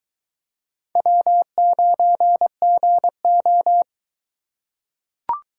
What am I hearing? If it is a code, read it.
Morse code, heard as a keyed tone, about 21 wpm, 700 Hz. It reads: W9GO